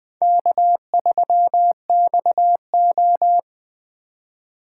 Morse code, heard as a keyed tone, about 20 wpm, 700 Hz. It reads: K3XO